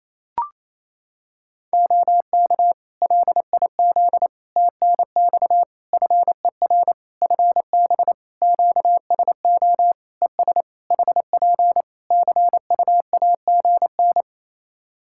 Morse code, heard as a keyed tone, about 28 wpm, 700 Hz. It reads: OK LIZ TNX FER FB QSO ES HP CUAGN
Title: Practice QSO - Exchange 4 - Ending - 28wpm